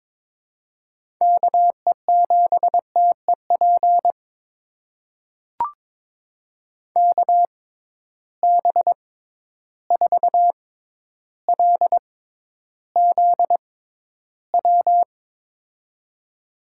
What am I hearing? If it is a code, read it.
Morse code, heard as a keyed tone, about 21 wpm, spaced as for 11 wpm, 700 Hz. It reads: KE7TEP KB4LZW
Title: USA Call Signs - 6 Characters - 6x Character Spacing - 22wpm